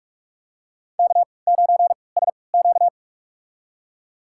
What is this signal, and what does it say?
Morse code, heard as a keyed tone, about 41 wpm, spaced as for 25 wpm, 690 Hz. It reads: K9SQ
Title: USA Call Signs - 4 Characters - 3x Character Spacing - 45wpm